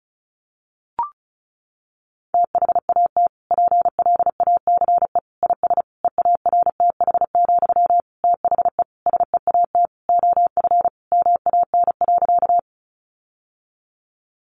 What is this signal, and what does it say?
Morse code, heard as a keyed tone, about 34 wpm, 700 Hz. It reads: THAT PLACE IS EARTH, THE SEAT OF MAN.
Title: Sentences from Top 1500 Words 35wpm